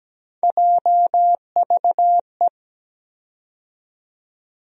Morse code, heard as a keyed tone, about 17 wpm, 700 Hz. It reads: JVE
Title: USA Call Sign Postfixes - 3 Letters 17wpm